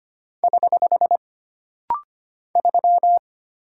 Morse code, heard as a keyed tone, about 25 wpm, 700 Hz. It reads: <HH> 3
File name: Single Character (Letter:Number:Punctuation:Prosign) - Rapid-Fire - 25wpm